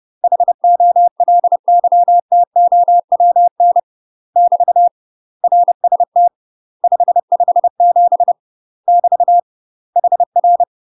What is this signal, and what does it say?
Morse code, heard as a keyed tone, about 30 wpm, 700 Hz. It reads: HOLYTOWN <BT> RST 557 <BT> HR
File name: UK MASTER QSOs TRAINING / 30WPM / UK MASTER QSOs 0013